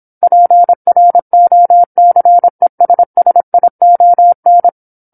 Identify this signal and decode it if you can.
Morse code, heard as a keyed tone, about 27 wpm, 700 Hz. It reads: PROCESSION
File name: 08-Chapter-08 26 WPM